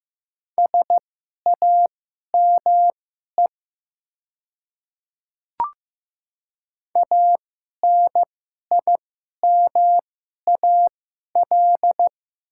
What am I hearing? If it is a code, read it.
Morse code, heard as a keyed tone, about 15 wpm, spaced as for 11 wpm, 700 Hz. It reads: SAME ANIMAL
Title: 2x Character Spacing - Top 300 Words - 15wpm